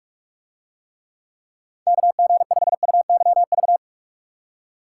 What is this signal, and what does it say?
Morse code, heard as a keyed tone, about 44 wpm, 700 Hz. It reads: KG5UYV